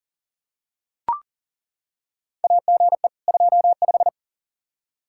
Morse code, heard as a keyed tone, about 40 wpm, 700 Hz. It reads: AGE 25
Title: Practice QSO - Exchange 3 - More Info - 40wpm